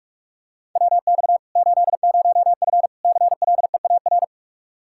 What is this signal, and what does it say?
Morse code, heard as a keyed tone, about 46 wpm, 700 Hz. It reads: WX 80F CLEAR
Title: Practice QSO - Exchange 2 - Info - 45wpm